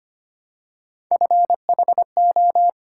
Morse code, heard as a keyed tone, about 25 wpm, 700 Hz. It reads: FHO
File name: Three Letter-Number Combinations - 25wpm